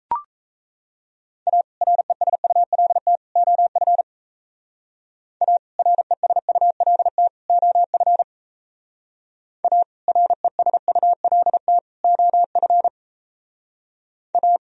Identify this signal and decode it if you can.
Morse code, heard as a keyed tone, about 35 wpm, 700 Hz. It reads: A RESULT OF A RESULT OF A RESULT OF A